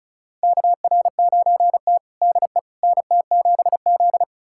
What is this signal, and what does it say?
Morse code, heard as a keyed tone, about 34 wpm, 700 Hz. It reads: KR9T DE NT7Z
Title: USA Callsign DE Callsign - 4 Characters - 35wpm